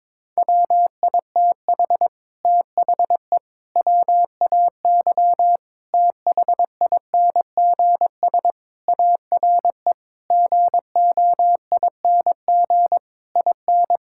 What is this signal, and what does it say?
Morse code, heard as a keyed tone, about 22 wpm, 700 Hz. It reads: WITH THE WAY THINGS ARE GOING IN
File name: practice / 22wpm / Sets of 8 Words - 22wpm